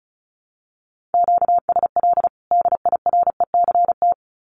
Morse code, heard as a keyed tone, about 34 wpm, 700 Hz. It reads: QSL DIRECT
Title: Common QSO Phrases 35wpm